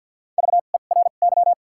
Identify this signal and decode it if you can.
Morse code, heard as a keyed tone, about 54 wpm, spaced as for 37 wpm, 700 Hz. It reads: VERY
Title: 2x Character Spacing - Top 200 Words - 50wpm